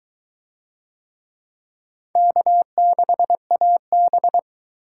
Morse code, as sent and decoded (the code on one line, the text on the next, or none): -.- -.... .- -...
K6AB